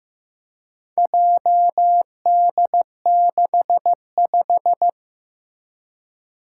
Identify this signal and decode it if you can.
Morse code, heard as a keyed tone, about 15 wpm, 700 Hz. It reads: JD65
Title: Four Challenging Letter-Number Combinations 15wpm